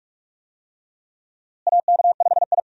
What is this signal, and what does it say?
Morse code, heard as a keyed tone, about 44 wpm, 700 Hz. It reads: AK5I
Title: USA Call Signs - 4 Characters - Repeated 3x - 45wpm